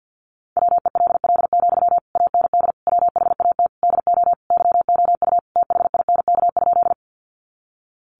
Morse code, heard as a keyed tone, about 48 wpm, 700 Hz. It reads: WELL, AND WHAT DO YOU THINK?
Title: Sentences from Top 200 Words 50wpm